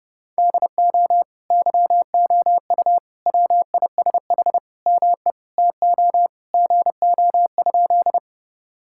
Morse code, heard as a keyed tone, about 30 wpm, 700 Hz. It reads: DO YOU WISH ME TO GO?
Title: Sentences from Top 1600 Words 30wpm